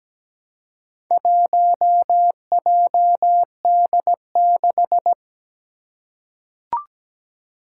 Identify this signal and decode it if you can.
Morse code, heard as a keyed tone, about 17 wpm, 700 Hz. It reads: 1JD6
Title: Four Challenging Letter-Number Combinations 17wpm